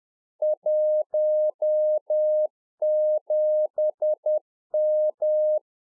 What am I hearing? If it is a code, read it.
Morse code, heard as a keyed tone, about 10 wpm, 610 Hz. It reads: 17M
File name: 2011-04-mp3 / UNFOGETTABLE QSO 10WPM